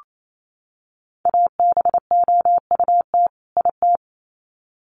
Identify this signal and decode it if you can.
Morse code, heard as a keyed tone, about 28 wpm, 700 Hz. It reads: ABOUT IT
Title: Sets of 2 Words 28wpm